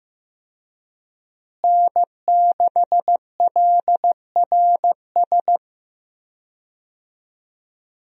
Morse code, heard as a keyed tone, about 15 wpm, 700 Hz. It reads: N6LRS
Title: USA Call Signs - 5 Characters - Repeated 3x - 15wpm